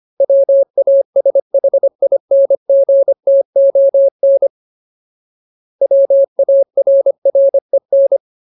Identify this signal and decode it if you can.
Morse code, heard as a keyed tone, about 25 wpm, 550 Hz. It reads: WASHINGTON WARREN
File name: Ultimate Word List / 25WPM Ultimate